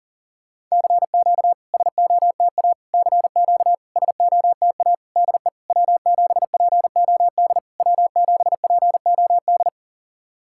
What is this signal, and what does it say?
Morse code, heard as a keyed tone, about 40 wpm, 700 Hz. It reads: CQ SOTA CQ SOTA DE W7POD W7POD
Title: SOTA QSO Practice 40wpm